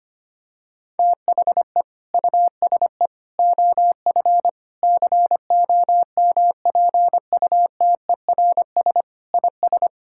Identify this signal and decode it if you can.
Morse code, heard as a keyed tone, about 25 wpm, 700 Hz. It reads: THE USE OF COMPUTERS IS
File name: practice / 25wpm / Sentences with 3-Word Phrases - 25wpm